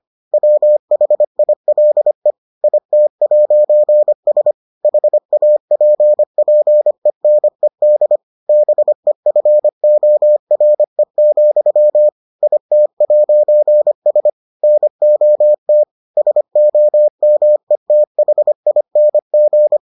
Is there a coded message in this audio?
Yes: Morse code from 0.3 to 19.8 s, heard as a keyed tone, about 25 wpm, 600 Hz. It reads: WHILE IT'S HAPPENED BEFORE, IT'S NOT SOMETHING